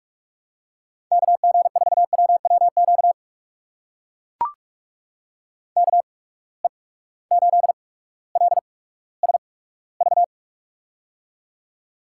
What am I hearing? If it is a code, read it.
Morse code, heard as a keyed tone, about 44 wpm, spaced as for 19 wpm, 700 Hz. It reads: KG4WWQ KE8LSV
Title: USA Call Signs - 6 Characters - 8x Character Spacing - 45wpm